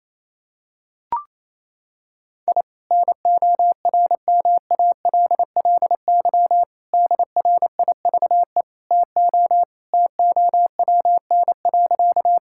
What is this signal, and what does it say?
Morse code, heard as a keyed tone, about 28 wpm, 700 Hz. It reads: I NORMALLY DRIVE TO TOWN.
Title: Sentences from Top 1000 Words - Review Part 6 - 28wpm